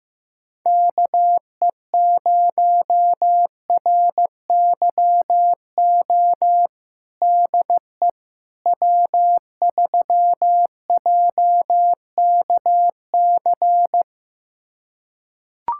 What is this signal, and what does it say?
Morse code, heard as a keyed tone, about 16 wpm, 700 Hz. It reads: KE0RYO DE W3JKC
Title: USA Callsign DE Callsign 15wpm